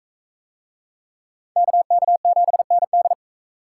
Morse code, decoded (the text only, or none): KK7ND